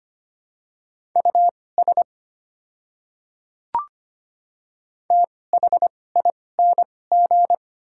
Morse code, heard as a keyed tone, about 25 wpm, spaced as for 18 wpm, 700 Hz. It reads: US THING
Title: 2x Character Spacing - Top 200 Words - 25wpm